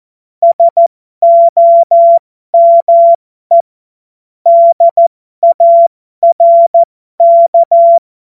Morse code, heard as a keyed tone, about 14 wpm, spaced as for 12 wpm, 680 Hz. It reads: SOME DARK
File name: Morsekurs / Witze